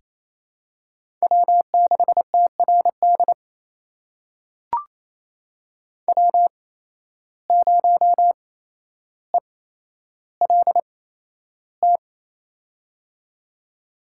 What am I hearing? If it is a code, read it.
Morse code, heard as a keyed tone, about 28 wpm, spaced as for 12 wpm, 700 Hz. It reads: W6TRD W0ELT
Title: USA Call Signs - 8x Character Spacing - 28wpm